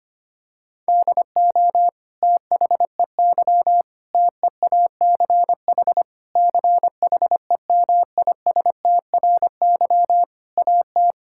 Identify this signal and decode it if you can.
Morse code, heard as a keyed tone, about 25 wpm, 700 Hz. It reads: DO THEY TEACH CHEMISTRY AT
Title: Sentences from Top 1000 Words - Review Part 6 - 25wpm